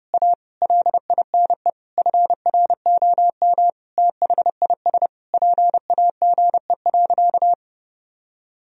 Morse code, heard as a keyed tone, about 30 wpm, 700 Hz. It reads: A LINE FROM THIS PAGE.
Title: Sentences from Top 1000 Words - Review Part 1 - 30wpm